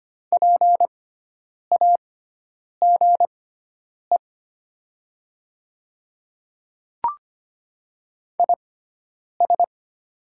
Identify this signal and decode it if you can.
Morse code, heard as a keyed tone, about 25 wpm, spaced as for 9 wpm, 700 Hz. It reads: PAGE IS
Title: 6x Character Spacing - Top 500 Words - 25wpm